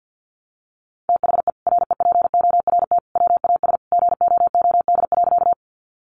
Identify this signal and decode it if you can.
Morse code, heard as a keyed tone, about 50 wpm, 700 Hz. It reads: THE REPORT WAS GOOD.
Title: Sentences from Top 900 Words 50wpm